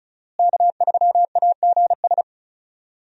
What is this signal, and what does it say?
Morse code, heard as a keyed tone, about 36 wpm, 700 Hz. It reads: K3AGS
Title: USA Call Signs - 5 Characters - Repeated 3x - 35wpm